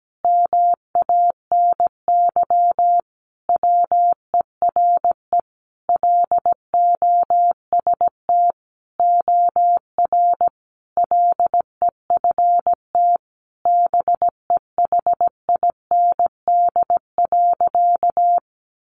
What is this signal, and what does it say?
Morse code, heard as a keyed tone, about 17 wpm, 700 Hz. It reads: MANY WERE LOST OR LEFT BEHIND.
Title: Sentences from Top 1400 Words 17wpm